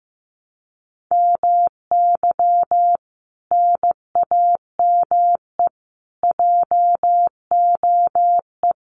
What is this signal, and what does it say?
Morse code, heard as a keyed tone, about 15 wpm, 700 Hz. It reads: MY NAME JOE
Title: Common QSO Phrases 15wpm